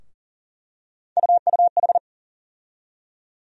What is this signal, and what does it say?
Morse code, heard as a keyed tone, about 40 wpm, 700 Hz. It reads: UUH
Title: USA Call Sign Postfixes - 3 Letters 40wpm